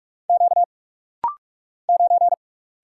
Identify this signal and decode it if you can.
Morse code, heard as a keyed tone, about 46 wpm, 700 Hz. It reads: Q 9
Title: Single Letter-Number Alt - Rapid-Fire 45wpm